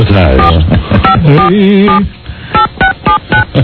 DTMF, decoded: 45478679